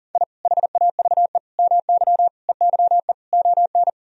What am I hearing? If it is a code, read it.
Morse code, heard as a keyed tone, about 40 wpm, 700 Hz. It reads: I HAVE MY EYE ON
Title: Sentences from Top 1000 Words - Review Part 1 - 40wpm